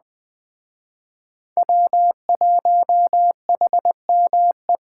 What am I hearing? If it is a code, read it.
Morse code, heard as a keyed tone, about 20 wpm, 700 Hz. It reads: W1HME